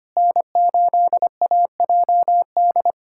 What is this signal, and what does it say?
Morse code, heard as a keyed tone, about 25 wpm, 700 Hz. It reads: N8AJD
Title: USA Callsign DE Callsign - 5 Characters - 25wpm